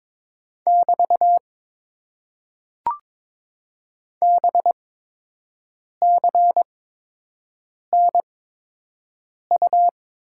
Morse code, heard as a keyed tone, about 22 wpm, spaced as for 6 wpm, 700 Hz. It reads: <BT> BCNU
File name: Top 90 QSO Elements - 8x Character Spacing - 22wpm